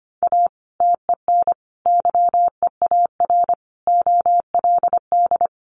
Morse code, heard as a keyed tone, about 25 wpm, 700 Hz. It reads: A TEN YEAR OLD
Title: Sentences from Top 1300 Words 25wpm